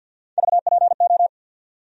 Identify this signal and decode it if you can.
Morse code, heard as a keyed tone, about 52 wpm, 700 Hz. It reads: VPO
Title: USA Call Sign Postfixes - 3 Letters 50wpm